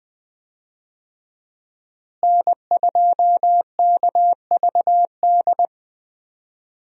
Morse code, heard as a keyed tone, about 20 wpm, 700 Hz. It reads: N2KVD
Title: USA Call Signs - 5 Characters - Repeated 2x - 20wpm